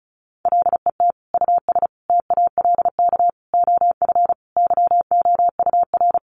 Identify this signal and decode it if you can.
Morse code, heard as a keyed tone, about 34 wpm, 700 Hz. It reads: LET US TALK OF YOUR
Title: Sentences from Top 1000 Words 35wpm